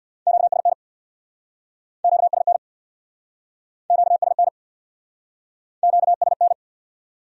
Morse code, heard as a keyed tone, about 64 wpm, 700 Hz. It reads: QSN QSN QSN QSN